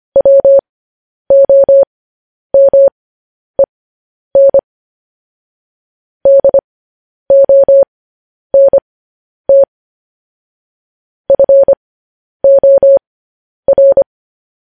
Morse code, heard as a keyed tone, about 25 wpm, spaced as for 10 wpm, 550 Hz. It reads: WOMEN DONT FOR
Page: news-headlines-in-morse-code-at-10-wpm